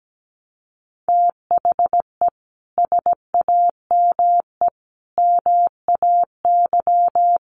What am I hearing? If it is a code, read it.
Morse code, heard as a keyed tone, about 17 wpm, 700 Hz. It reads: THE SAME MAY
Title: Sentences from Top 1100 Words 17wpm